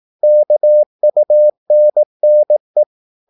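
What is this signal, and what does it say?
Morse code, heard as a keyed tone, about 18 wpm, 600 Hz. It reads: KUNNE
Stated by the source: Slow (QRS) Morse practice at 18 wpm, CW med Gnister.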